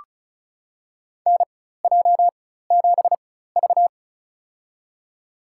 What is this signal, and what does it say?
Morse code, heard as a keyed tone, about 34 wpm, spaced as for 16 wpm, 700 Hz. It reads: NJ7V
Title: USA Call Signs - 4 Characters - 4x Character Spacing - 35wpm